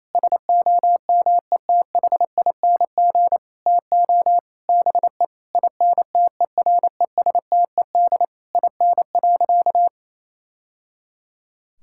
Morse code, heard as a keyed tone, about 28 wpm, 690 Hz. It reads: SOMETHING TO BE INTERESTED IN.
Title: Sentences from Top 1000 Words - Review Part 6 - 28wpm